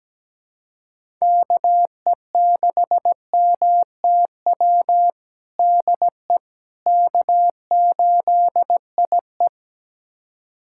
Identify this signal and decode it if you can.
Morse code, heard as a keyed tone, about 17 wpm, 700 Hz. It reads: KE6MTW DE K8IE
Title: USA Callsign DE Callsign 17wpm